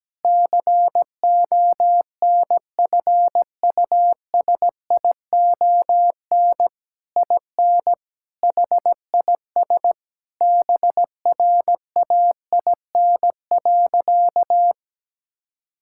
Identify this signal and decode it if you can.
Morse code, heard as a keyed tone, about 17 wpm, 700 Hz. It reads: CONFUSION IN HIS BRAIN.
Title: Sentences from Top 1600 Words 17wpm